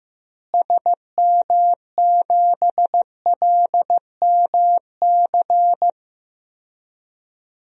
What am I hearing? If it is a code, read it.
Morse code, heard as a keyed tone, about 15 wpm, 700 Hz. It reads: SM7LMC
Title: DXCC-100 Call Signs - 6 Characters - Repeated 2x - 15wpm